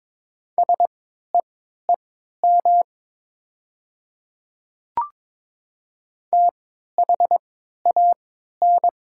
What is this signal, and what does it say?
Morse code, heard as a keyed tone, about 22 wpm, spaced as for 13 wpm, 700 Hz. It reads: SEEM THAN
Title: 3x Character Spacing - Top 500 Words - 22wpm